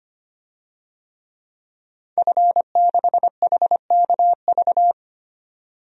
Morse code, heard as a keyed tone, about 25 wpm, 700 Hz. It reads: F6HKV
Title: DXCC-100 Call Signs - 5 Characters - 25wpm